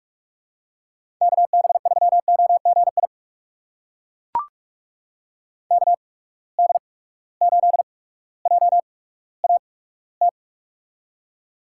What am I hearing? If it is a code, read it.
Morse code, heard as a keyed tone, about 42 wpm, spaced as for 19 wpm, 700 Hz. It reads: KB3OGI KD8JAT
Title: USA Call Signs - 6 Characters - 8x Character Spacing - 45wpm